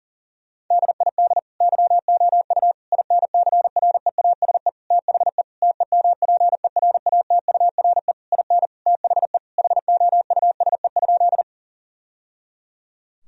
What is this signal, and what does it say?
Morse code, heard as a keyed tone, about 40 wpm, 700 Hz. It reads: DID YOU INCREASE THE TEMPERATURE IN THE HOUSE?
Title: Sentences from Top 1000 Words - Review Part 4 - 40wpm